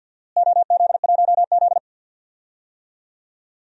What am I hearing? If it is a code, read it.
Morse code, heard as a keyed tone, about 46 wpm, 700 Hz. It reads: OZ1Z
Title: DXCC-100 Call Signs - 4 Characters - Repeated 2x - 50wpm